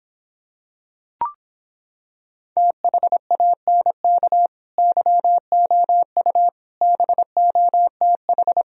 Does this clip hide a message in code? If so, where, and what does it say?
Yes: Morse code from 2.6 to 8.6 s, heard as a keyed tone, about 26 wpm, 700 Hz. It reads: THANK YOU BOTH